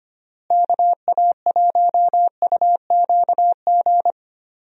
Morse code, heard as a keyed tone, about 25 wpm, 700 Hz. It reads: KA1UQG
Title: USA Call Signs - 6 Characters - Repeated 3x - 25wpm